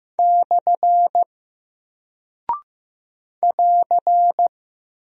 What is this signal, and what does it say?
Morse code, heard as a keyed tone, about 15 wpm, 700 Hz. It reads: / <AR>
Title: Single Character (Letter:Number:Punctuation:Prosign) Alt 15wpm